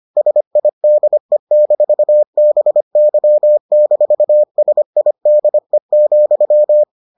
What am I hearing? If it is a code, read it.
Morse code, heard as a keyed tone, about 25 wpm, 600 Hz. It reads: SIDE-BY-SIDE,